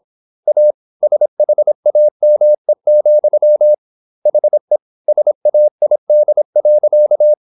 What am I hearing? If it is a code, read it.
Morse code, heard as a keyed tone, about 26 wpm, 600 Hz. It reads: A SHAME, HE SAID.
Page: Never 26wpm | CW med Gnister